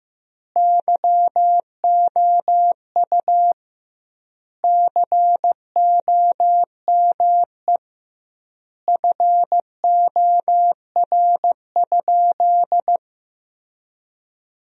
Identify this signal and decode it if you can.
Morse code, heard as a keyed tone, about 15 wpm, 700 Hz. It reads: YOU COME FOR?
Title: 2x Word-Spacing - Sentences from Top 100 Words - 15wpm